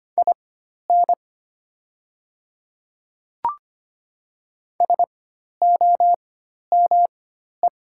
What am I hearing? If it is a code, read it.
Morse code, heard as a keyed tone, about 25 wpm, spaced as for 12 wpm, 690 Hz. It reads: IN SOME
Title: 4x Character Spacing - Top 100 Words - 25wpm